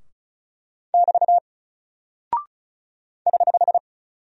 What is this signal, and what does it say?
Morse code, heard as a keyed tone, about 36 wpm, 700 Hz. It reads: <BT> <HH>
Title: Single Character (Letter:Number:Punctuation:Prosign) Alt 35wpm